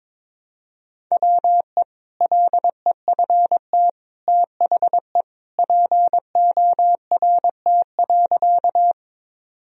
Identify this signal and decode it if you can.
Morse code, heard as a keyed tone, about 22 wpm, 700 Hz. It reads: WE LEFT THE PORT.
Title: Sentences from Top 1000 Words - Review Part 1 - 22wpm